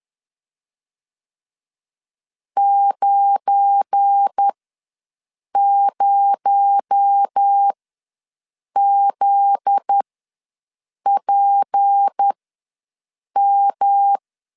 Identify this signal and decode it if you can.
Morse code, heard as a keyed tone, about 10 wpm, spaced as for 6 wpm, 800 Hz. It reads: 90ZPM
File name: LEZIONI / 7 GF9Z / 7 GF9Z ws 10wpm 3919